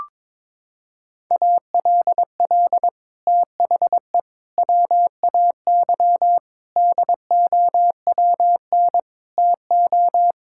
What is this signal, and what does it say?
Morse code, heard as a keyed tone, about 22 wpm, 700 Hz. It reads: ALL THE WAY DOWN TO
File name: practice / 22wpm / Sets of 6 Words - 22wpm